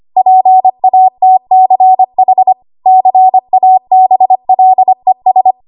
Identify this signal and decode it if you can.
Morse code, heard as a keyed tone, about 25 wpm, 750 Hz. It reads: PATCH CABLES
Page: W1AW Web Code Practice Files